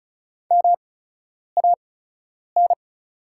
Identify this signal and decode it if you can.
Morse code, heard as a keyed tone, about 36 wpm, spaced as for 10 wpm, 700 Hz. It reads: MAN
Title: 8x Character Spacing - Top 500 Words - 35wpm